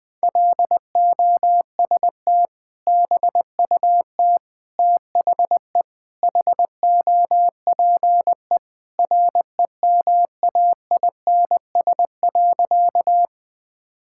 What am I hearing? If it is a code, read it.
Morse code, heard as a keyed tone, about 21 wpm, 700 Hz. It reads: LOST BUT THE HOPE REMAINS.
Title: Sentences with 2-Word Phrases - 20wpm